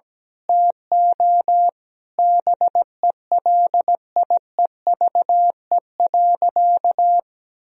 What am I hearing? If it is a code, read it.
Morse code, heard as a keyed tone, about 17 wpm, 700 Hz. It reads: TO BELIEVE.
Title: Sentences from Top 1000 Words - Review Part 4 - 17wpm